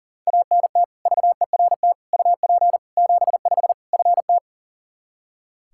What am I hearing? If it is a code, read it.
Morse code, heard as a keyed tone, about 40 wpm, 700 Hz. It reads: ANT VERT UP 75 FT
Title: K1USN SST QSO Practice 40wpm